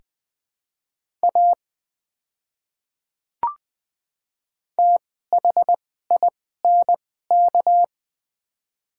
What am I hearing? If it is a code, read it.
Morse code, heard as a keyed tone, about 20 wpm, spaced as for 15 wpm, 700 Hz. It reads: A THINK
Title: 2x Character Spacing - Top 200 Words - 20wpm